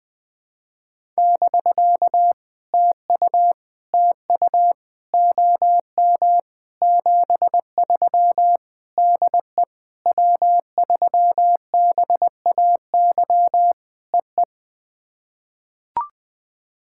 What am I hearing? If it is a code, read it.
Morse code, heard as a keyed tone, about 20 wpm, 690 Hz. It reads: <BK> TU TU OM 73 DE W3BAY EE
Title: POTA QSO Practice - 20wpm